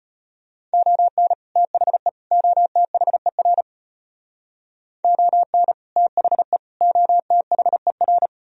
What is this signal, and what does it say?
Morse code, heard as a keyed tone, about 30 wpm, 700 Hz. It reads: ON THE OTHER ON THE OTHER
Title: Speed-Racing - Sets of 3 Words 25wpm